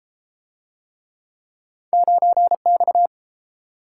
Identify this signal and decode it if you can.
Morse code, heard as a keyed tone, about 31 wpm, 700 Hz. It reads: 9X